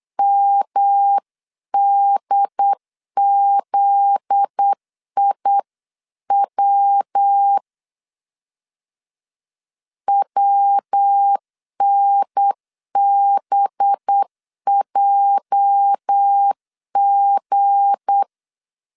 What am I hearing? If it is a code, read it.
Morse code, heard as a keyed tone, about 8 wpm, 800 Hz. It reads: MDZIW WNBJG
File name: LEZIONI / 9 2 8wpm 3437